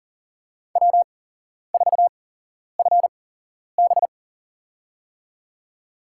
Morse code, heard as a keyed tone, about 38 wpm, spaced as for 11 wpm, 690 Hz. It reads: W4FB
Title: USA Call Signs - 4 Characters - 8x Character Spacing - 40wpm